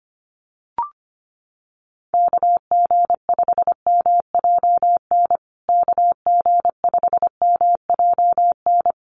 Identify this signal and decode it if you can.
Morse code, heard as a keyed tone, about 25 wpm, 700 Hz. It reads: KG5MJN KG5MJN
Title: USA Call Signs Repeated 2x 25wpm